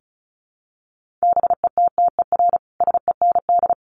Morse code, heard as a keyed tone, about 34 wpm, 700 Hz. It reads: BETTER SEND
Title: Sentences from Top 1500 Words 35wpm